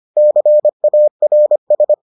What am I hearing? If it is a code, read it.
Morse code, heard as a keyed tone, about 25 wpm, 600 Hz. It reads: CARS